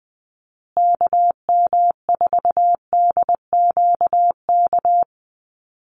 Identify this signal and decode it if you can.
Morse code, heard as a keyed tone, about 20 wpm, 700 Hz. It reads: KM4DQK
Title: USA Call Signs Repeated 2x 20wpm